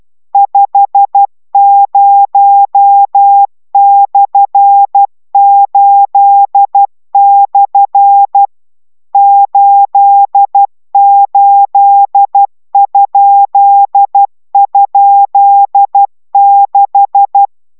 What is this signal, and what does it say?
Morse code, heard as a keyed tone, about 12 wpm, 800 Hz. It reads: 50/8/ 88??6